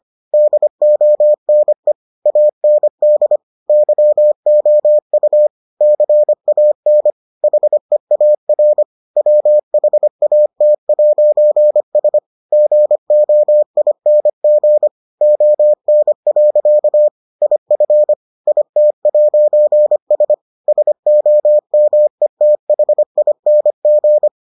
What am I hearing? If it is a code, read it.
Morse code, heard as a keyed tone, about 25 wpm, 600 Hz. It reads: DONE AND YOU CAN HEAR WHAT'S GOING ON. IF IT'S SOMETHING